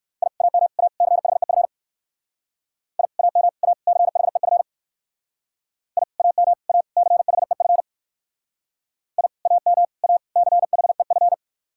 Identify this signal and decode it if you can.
Morse code, heard as a keyed tone, about 59 wpm, 700 Hz. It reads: I AM A CHEF I AM A CHEF I AM A CHEF I AM A CHEF